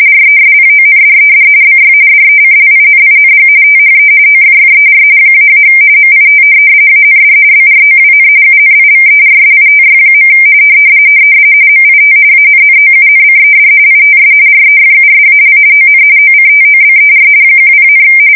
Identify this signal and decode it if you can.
RTTY, about 45.45 baud. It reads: RESPONDED BY DECLINING THE OFFER -- MIT AND BROWN UNIVERSITY. MIT ANNOUNCED ITS REFUSAL ON OCTOBER 10,
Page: 170 Shift 2125/2295